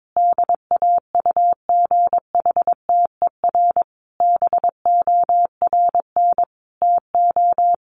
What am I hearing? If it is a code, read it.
Morse code, heard as a keyed tone, about 22 wpm, 700 Hz. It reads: DAUGHTER BORN TO
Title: Sentences from Top 1500 Words 22wpm